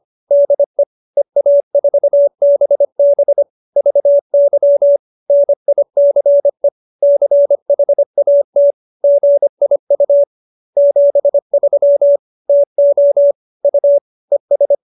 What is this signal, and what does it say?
Morse code, heard as a keyed tone, about 25 wpm, 570 Hz. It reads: DE EA4BB VY NICE CHAT GIU 73 TO U ES